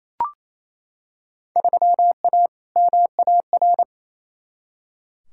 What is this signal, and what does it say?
Morse code, heard as a keyed tone, about 28 wpm, 700 Hz. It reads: 3A MAR
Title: Field Day Exchanges Encore - 28wpm